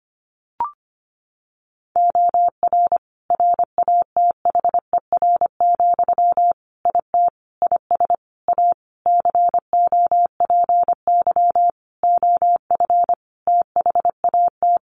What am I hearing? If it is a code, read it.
Morse code, heard as a keyed tone, about 25 wpm, 700 Hz. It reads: OR RATHER, IT IS A COPY OF THAT
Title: Sentences from Top 1400 Words 25wpm